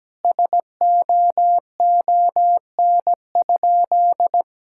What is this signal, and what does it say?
Morse code, heard as a keyed tone, about 17 wpm, 700 Hz. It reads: SOON?
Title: Sentences from Top 1600 Words 17wpm